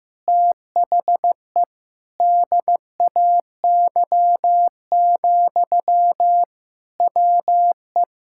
Morse code, heard as a keyed tone, about 15 wpm, 700 Hz. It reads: THE DAY, WE
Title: Sentences from Top 1000 Words - Review Part 5 - 15wpm